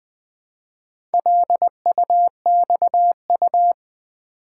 Morse code, heard as a keyed tone, about 20 wpm, 700 Hz. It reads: LUXU
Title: Build A Word - 2 Letters At A Time - 6-Letter Words - 20wpm